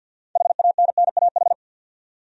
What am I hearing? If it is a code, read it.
Morse code, heard as a keyed tone, about 50 wpm, 700 Hz. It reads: HANNAH
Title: USA Common Names - 50wpm